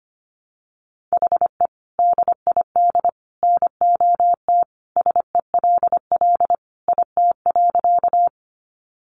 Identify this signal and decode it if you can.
Morse code, heard as a keyed tone, about 25 wpm, 700 Hz. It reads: HE DID NOT SELL IT.
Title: Sentences from Top 1100 Words 25wpm